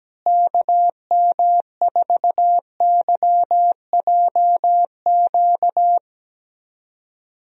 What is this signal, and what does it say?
Morse code, heard as a keyed tone, about 17 wpm, 700 Hz. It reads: KM4YJQ
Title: USA Call Signs - 6 Characters - Repeated 3x - 17wpm